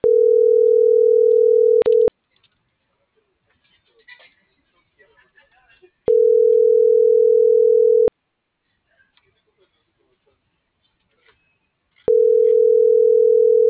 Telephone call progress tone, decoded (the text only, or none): ringback tone